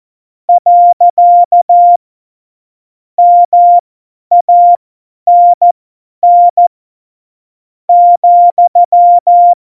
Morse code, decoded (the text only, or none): . MANN ,